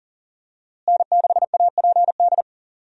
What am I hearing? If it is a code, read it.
Morse code, heard as a keyed tone, about 40 wpm, 700 Hz. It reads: N6APD
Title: USA Call Signs - 5 Characters - Repeated 2x - 40wpm